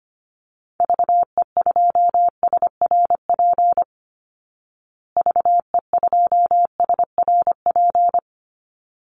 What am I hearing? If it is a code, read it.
Morse code, heard as a keyed tone, about 25 wpm, 700 Hz. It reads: VE2SRP VE2SRP
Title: Call Signs Repeated 3x - 25wpm